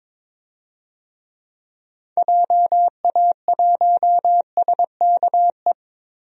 Morse code, heard as a keyed tone, about 22 wpm, 700 Hz. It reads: JA1SKE
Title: DXCC-100 Call Signs - 6 Characters - 22wpm